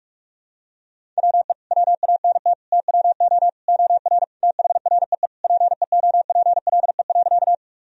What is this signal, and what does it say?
Morse code, heard as a keyed tone, about 42 wpm, 700 Hz. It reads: WE WANT TWO OR THREE PEOPLE.
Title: Sentences from Top 1000 Words - Review Part 1 - 45wpm